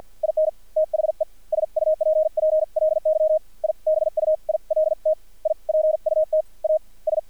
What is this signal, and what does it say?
Morse code, heard as a keyed tone, about 50 wpm, spaced as for 43 wpm, 630 Hz. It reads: IN THE SUPPLY I BUILT I PUT A S
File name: archived / CRSnet